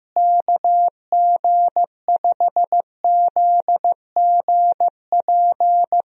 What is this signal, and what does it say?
Morse code, heard as a keyed tone, about 15 wpm, 700 Hz. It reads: KG5ZGP
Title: USA Call Signs - 6 Characters - Repeated 3x - 15wpm